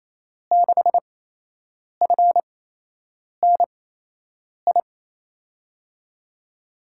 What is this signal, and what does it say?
Morse code, heard as a keyed tone, about 28 wpm, spaced as for 8 wpm, 700 Hz. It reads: 6FNI
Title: USA Call Signs - 8x Character Spacing - 28wpm